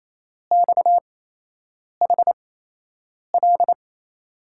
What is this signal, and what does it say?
Morse code, heard as a keyed tone, about 28 wpm, spaced as for 8 wpm, 700 Hz. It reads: XHL
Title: USA Call Signs - 8x Character Spacing - 28wpm